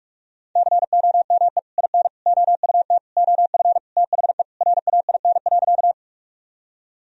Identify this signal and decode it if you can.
Morse code, heard as a keyed tone, about 44 wpm, 700 Hz. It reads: COME IN OUT OF THE RAIN.
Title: Sentences from Top 1000 Words - Review Part 1 - 45wpm